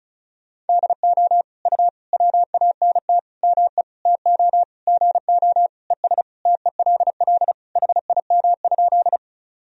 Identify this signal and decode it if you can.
Morse code, heard as a keyed tone, about 34 wpm, 690 Hz. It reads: DO U WANT ME TO GO ES TELL HIM?
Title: Sentences from Top 200 Words and Abbreviations 35wpm